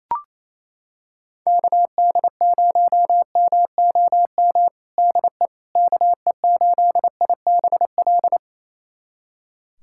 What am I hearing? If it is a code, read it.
Morse code, heard as a keyed tone, about 28 wpm, 700 Hz. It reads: KD0MOM DE KE8IBL
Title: USA Callsign DE Callsign - 6 Characters - 28wpm